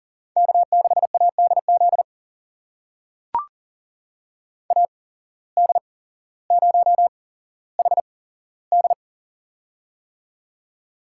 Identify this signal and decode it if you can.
Morse code, heard as a keyed tone, about 40 wpm, spaced as for 17 wpm, 700 Hz. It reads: K6ADZ AD0HD
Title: USA Call Signs - 5 Characters - 8x Character Spacing - 40wpm